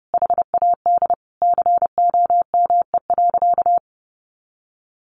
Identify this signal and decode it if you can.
Morse code, heard as a keyed tone, about 30 wpm, 700 Hz. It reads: HAD COME.
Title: Sentences from Top 100 Words 30wpm